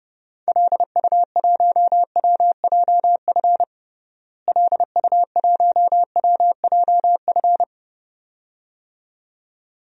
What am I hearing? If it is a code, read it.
Morse code, heard as a keyed tone, about 32 wpm, 700 Hz. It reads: LU1WJF LU1WJF
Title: DXCC-100 Call Signs - 6 Characters - Repeated 2x - 30wpm